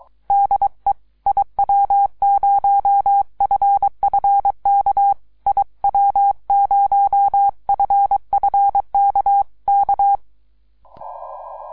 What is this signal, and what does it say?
Morse code, heard as a keyed tone, about 23 wpm, 800 Hz. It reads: DE IW0FFK IW0FFK K
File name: iw0ffk / 2012 04 02 3B8DB IW0FFK